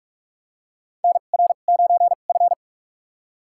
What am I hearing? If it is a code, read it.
Morse code, heard as a keyed tone, about 44 wpm, spaced as for 32 wpm, 700 Hz. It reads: NR9F
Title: USA Call Signs - 4 Characters - 2x Character Spacing - 45wpm